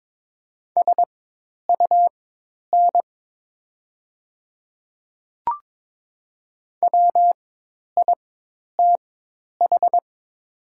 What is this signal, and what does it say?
Morse code, heard as a keyed tone, about 22 wpm, spaced as for 10 wpm, 700 Hz. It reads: SUN WITH
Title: 4x Character Spacing - Top 300 Words - 22wpm